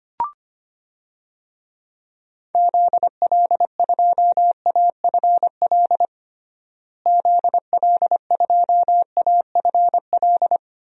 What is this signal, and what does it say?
Morse code, heard as a keyed tone, about 25 wpm, 700 Hz. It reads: ZL2AFL ZL2AFL
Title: DXCC-100 Call Signs - 6 Characters - Repeated 2x - 25wpm